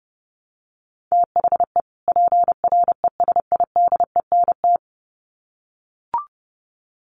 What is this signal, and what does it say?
Morse code, heard as a keyed tone, about 30 wpm, 700 Hz. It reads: THE PRESIDENT
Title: Sets of 2 Words 30wpm